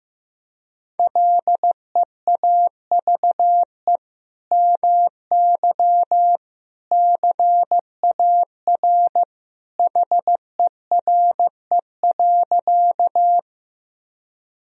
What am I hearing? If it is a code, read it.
Morse code, heard as a keyed tone, about 15 wpm, 700 Hz. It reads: LEAVE MY CAR HERE.
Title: Sentences from Top 1000 Words - Review Part 1 - 15wpm